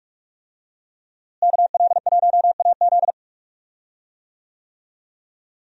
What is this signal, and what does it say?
Morse code, heard as a keyed tone, about 46 wpm, 700 Hz. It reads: KL1AZ
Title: USA Call Signs - 5 Characters - Repeated 3x - 45wpm